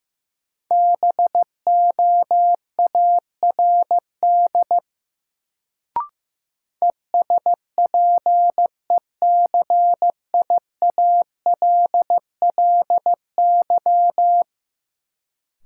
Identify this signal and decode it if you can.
Morse code, heard as a keyed tone, about 15 wpm, 700 Hz. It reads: BOARD ESPECIALLY
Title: Sentences from Top 700 Words - New Word Drill - 15wpm